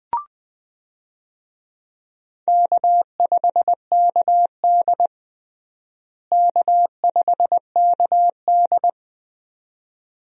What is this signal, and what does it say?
Morse code, heard as a keyed tone, about 20 wpm, 700 Hz. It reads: K5KD K5KD
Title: USA Call Signs - 4 Characters - Repeated 2x - 20wpm